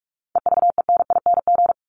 Morse code, heard as a keyed tone, about 44 wpm, 700 Hz. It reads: EVENING
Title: Top 1501-1750 Words 45wpm